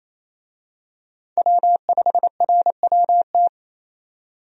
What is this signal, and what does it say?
Morse code, heard as a keyed tone, about 28 wpm, 700 Hz. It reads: W5RWT
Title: USA Call Signs - 5 Characters - 28wpm